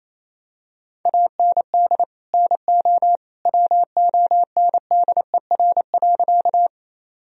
Morse code, heard as a keyed tone, about 28 wpm, 700 Hz. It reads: AND NO WONDER.
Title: Sentences from Top 1000 Words - Review Part 3 - 28wpm